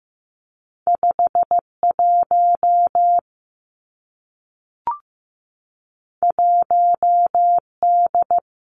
Morse code, heard as a keyed tone, about 15 wpm, 700 Hz. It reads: 5 1 1 D
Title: Two Challenging Letter-Number Combinations 15wpm